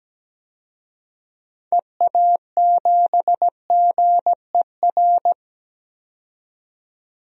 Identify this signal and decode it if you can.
Morse code, heard as a keyed tone, about 17 wpm, 700 Hz. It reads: EA7GER
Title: DXCC-100 Call Signs - 6 Characters - 17wpm